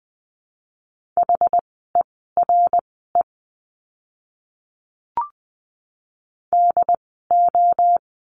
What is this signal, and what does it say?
Morse code, heard as a keyed tone, about 20 wpm, spaced as for 14 wpm, 700 Hz. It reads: HERE DO
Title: 192 Lesson 31f - Letter v - Words 20wpm